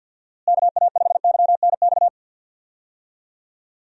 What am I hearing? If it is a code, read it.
Morse code, heard as a keyed tone, about 48 wpm, 700 Hz. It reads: KA5YNX
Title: USA Call Signs - 6 Characters - Repeated 2x - 50wpm